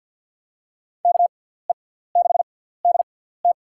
Morse code, heard as a keyed tone, about 46 wpm, spaced as for 17 wpm, 700 Hz. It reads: KE6DT
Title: USA Call Signs - 6x Character Spacing - 50wpm